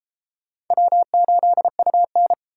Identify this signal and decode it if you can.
Morse code, heard as a keyed tone, about 32 wpm, 700 Hz. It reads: W8UN